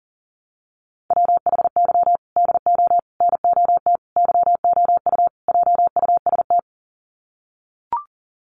Morse code, heard as a keyed tone, about 40 wpm, 700 Hz. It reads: WHY DO NOT YOU JUST
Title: Sets of 5 Words 40wpm